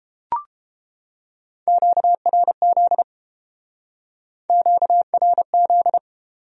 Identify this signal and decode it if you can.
Morse code, heard as a keyed tone, about 28 wpm, 700 Hz. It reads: QRZ QRZ